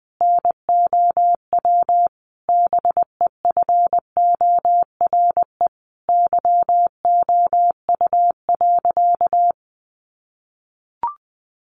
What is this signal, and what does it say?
Morse code, heard as a keyed tone, about 20 wpm, 700 Hz. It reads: NOW BEFORE YOU.
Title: Sentences from Top 1200 Words 20wpm